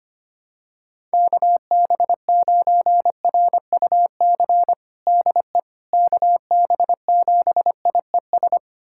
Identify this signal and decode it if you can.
Morse code, heard as a keyed tone, about 25 wpm, 700 Hz. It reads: KB9RUC DE KB7IES
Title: USA Callsign DE Callsign - 6 Characters - 25wpm